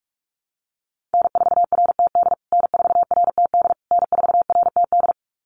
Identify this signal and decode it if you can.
Morse code, heard as a keyed tone, about 44 wpm, 690 Hz. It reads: N4RTD N4RTD N4RTD
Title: USA Call Signs Repeated 3x 45wpm